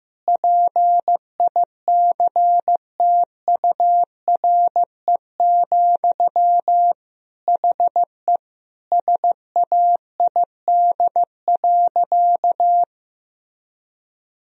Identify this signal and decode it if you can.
Morse code, heard as a keyed tone, about 15 wpm, 700 Hz. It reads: PICTURE, HE SAID.
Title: Sentences from Top 1000 Words - Review Part 1 - 15wpm